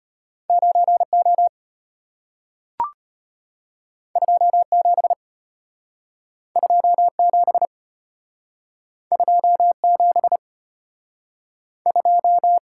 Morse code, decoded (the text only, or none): 9O 27 27 27 2